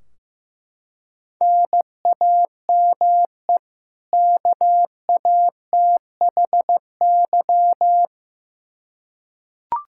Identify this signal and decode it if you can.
Morse code, heard as a keyed tone, about 16 wpm, 700 Hz. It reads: NAME KATHY
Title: Practice QSO - Exchange 1 - Beginning - 15wpm